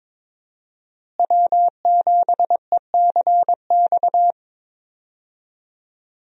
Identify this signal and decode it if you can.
Morse code, heard as a keyed tone, about 21 wpm, 700 Hz. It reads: W7ECX
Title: USA Call Signs - 5 Characters - Repeated 2x - 22wpm